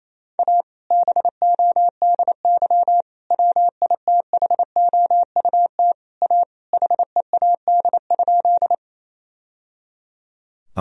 Morse code, heard as a keyed tone, about 27 wpm, 700 Hz. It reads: A BODY WITHOUT A HEAD?
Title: Sentences from Top 1000 Words - Review Part 3 - 28wpm